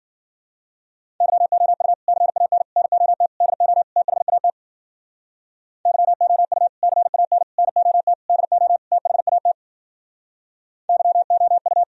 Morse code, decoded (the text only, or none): YOU CAN NOT DO THAT YOU CAN NOT DO THAT YOU